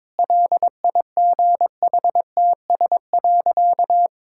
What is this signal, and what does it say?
Morse code, heard as a keyed tone, about 22 wpm, 700 Hz. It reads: LIGHTS.
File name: Difficult Sentences with Hints - 22wpm